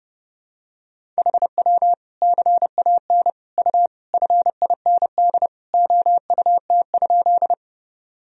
Morse code, heard as a keyed tone, about 30 wpm, 700 Hz. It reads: HW CAN U FIND OUT?
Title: Sentences from Top 200 Words and Abbreviations 30wpm